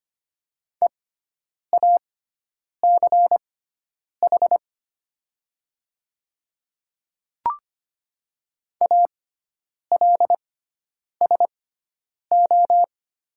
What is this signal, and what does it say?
Morse code, heard as a keyed tone, about 25 wpm, spaced as for 9 wpm, 700 Hz. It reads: EACH ALSO
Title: 6x Character Spacing - Top 200 Words - 25wpm